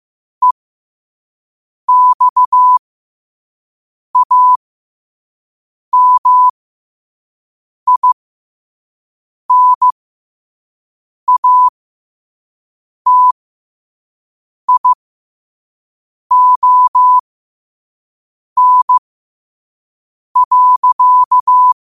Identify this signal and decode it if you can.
Morse code, heard as a keyed tone, about 15 wpm, spaced as for 5 wpm, 1000 Hz. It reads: EXAMINATION.
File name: qotdfiles / 5wpm